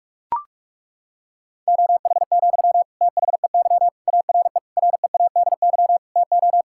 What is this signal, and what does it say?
Morse code, heard as a keyed tone, about 41 wpm, 700 Hz. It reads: OH, THEY ARE READY TO
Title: Sentences from Top 1600 Words 45wpm